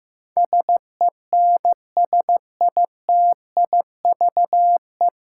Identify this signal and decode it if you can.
Morse code, heard as a keyed tone, about 16 wpm, 700 Hz. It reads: SENSITIVE
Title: Sentences from Top 1400 Words - New Word Drill Alt - 15wpm